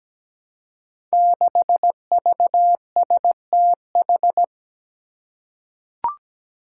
Morse code, heard as a keyed tone, about 17 wpm, 700 Hz. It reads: 6VSTH
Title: Five Challenging Letter-Number Combinations 17wpm